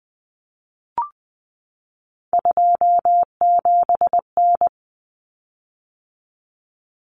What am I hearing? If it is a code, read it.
Morse code, heard as a keyed tone, about 20 wpm, 700 Hz. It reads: 27N
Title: Common Cut Numbers - 3-Digit 20wpm